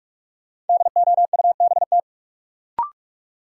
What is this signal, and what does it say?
Morse code, heard as a keyed tone, about 46 wpm, 700 Hz. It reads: DOUBT
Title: Sentences from Top 1500 Words - New Word Drill Alt - 45wpm